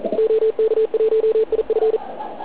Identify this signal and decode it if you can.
Morse code, heard as a keyed tone, about 42 wpm, 440 Hz, with radio noise. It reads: OK1IF